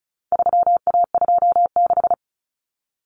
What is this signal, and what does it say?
Morse code, heard as a keyed tone, about 34 wpm, 700 Hz. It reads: 3A26
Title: Common Cut Numbers - 4-Digit 35wpm